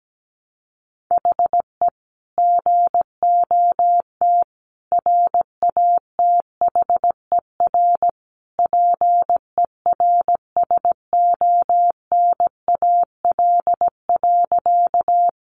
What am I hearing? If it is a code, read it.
Morse code, heard as a keyed tone, about 17 wpm, 700 Hz. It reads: HE GOT RATHER PERSONAL.
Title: Sentences from Top 700 Words 17wpm